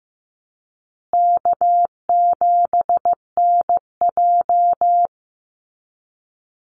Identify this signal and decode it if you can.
Morse code, heard as a keyed tone, about 15 wpm, 700 Hz. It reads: K7NJ
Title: Call Signs Repeated 2x - 15wpm